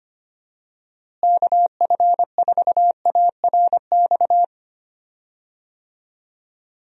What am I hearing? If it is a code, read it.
Morse code, heard as a keyed tone, about 25 wpm, 700 Hz. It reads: KF4ARX
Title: USA Call Signs - 6 Characters - Repeated 2x - 25wpm